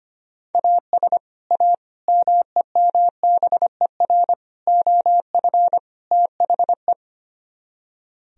Morse code, heard as a keyed tone, about 33 wpm, 700 Hz. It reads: AS A MEMBER OF THE